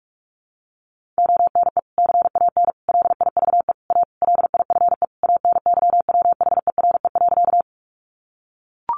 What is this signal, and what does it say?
Morse code, heard as a keyed tone, about 44 wpm, 700 Hz. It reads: ONE CAN LIVE A LIFE ANYWHERE.
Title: Sentences from Top 1400 Words 45wpm